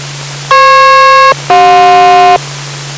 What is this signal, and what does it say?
Aircraft SELCAL, heard as two pairs of tones. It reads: FP-CJ